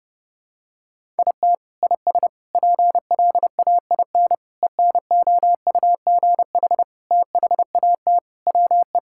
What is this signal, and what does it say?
Morse code, heard as a keyed tone, about 32 wpm, 700 Hz. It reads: IT IS PLAIN ENOUGH THAT WE
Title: Sentences from Top 1000 Words - Review Part 5 - 30wpm